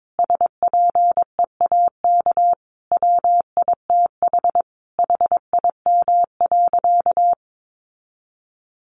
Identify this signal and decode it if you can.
Morse code, heard as a keyed tone, about 22 wpm, 700 Hz. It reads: SPEAK WITH HIM.
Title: Sentences from Top 1100 Words 22wpm